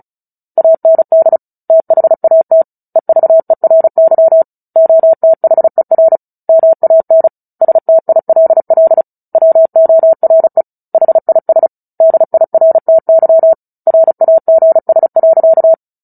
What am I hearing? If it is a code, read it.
Morse code, heard as a keyed tone, about 34 wpm, 650 Hz. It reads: AND THAT EVERY OTHER MAN STILL WORE HIS DIRTY RAGS.
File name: Book Two / 09-Chapter 9 - 35 WPM